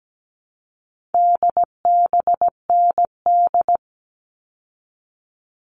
Morse code, heard as a keyed tone, about 17 wpm, 700 Hz. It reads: DBND
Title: All Cut Numbers - 4-Digit 17wpm